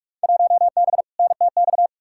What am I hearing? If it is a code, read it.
Morse code, heard as a keyed tone, about 41 wpm, 700 Hz. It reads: 1B NTX
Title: Field Day Exchanges - 45wpm